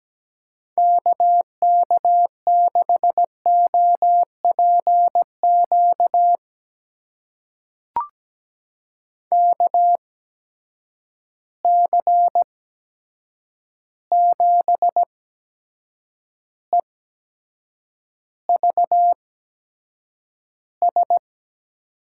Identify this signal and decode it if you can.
Morse code, heard as a keyed tone, about 17 wpm, spaced as for 7 wpm, 700 Hz. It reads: KK6OPQ KC7EVS
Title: USA Call Signs - 6 Characters - 8x Character Spacing - 17wpm